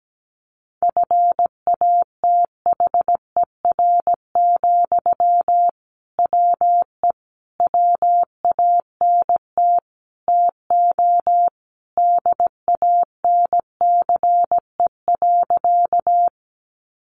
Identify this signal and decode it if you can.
Morse code, heard as a keyed tone, about 17 wpm, 700 Hz. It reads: FATHER, WE WANT TO DANCE.
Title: Sentences from Top 700 Words 17wpm